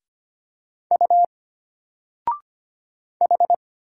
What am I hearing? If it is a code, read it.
Morse code, heard as a keyed tone, about 25 wpm, 700 Hz. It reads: U H
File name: Single-Letter-Number-25wpm